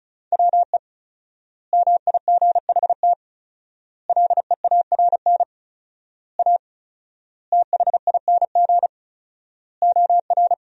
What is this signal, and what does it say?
Morse code, heard as a keyed tone, about 34 wpm, 700 Hz. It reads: WE MIGHT LEARN A THING OR
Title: 4x Word-Spacing - Sentences from Top 300 Words - 35wpm